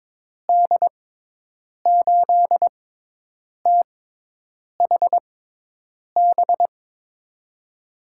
Morse code, heard as a keyed tone, about 22 wpm, spaced as for 8 wpm, 700 Hz. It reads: D8THB
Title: USA Call Signs - 6 Characters - 6x Character Spacing - 22wpm